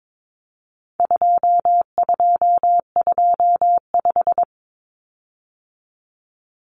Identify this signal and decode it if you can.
Morse code, heard as a keyed tone, about 22 wpm, 700 Hz. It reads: 2225
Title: Common Cut Numbers - 4-Digit 22wpm